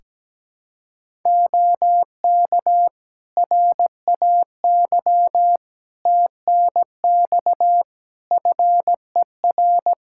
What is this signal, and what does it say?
Morse code, heard as a keyed tone, about 17 wpm, 700 Hz. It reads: OK RAY TNX FER
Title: Practice QSO - Exchange 4 - Ending - 17wpm